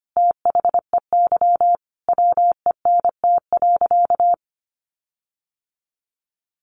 Morse code, heard as a keyed tone, about 25 wpm, 700 Hz. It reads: THEY WENT.
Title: Sentences from Top 1300 Words 25wpm